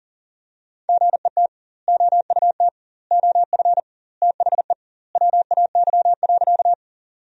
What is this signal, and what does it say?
Morse code, heard as a keyed tone, about 37 wpm, 700 Hz. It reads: GET OUT OF THE WAY.
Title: 2x Word-Spacing - Sentences from Top 200 Words - 40wpm